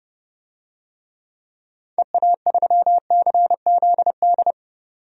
Morse code, heard as a keyed tone, about 30 wpm, 700 Hz. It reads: EA3CZD
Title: DXCC-100 Call Signs - 6 Characters - 30wpm